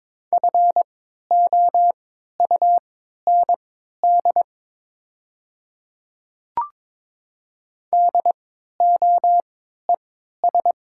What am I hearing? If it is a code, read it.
Morse code, heard as a keyed tone, about 23 wpm, spaced as for 13 wpm, 700 Hz. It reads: FOUND DOES
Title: 3x Character Spacing - Top 300 Words - 22wpm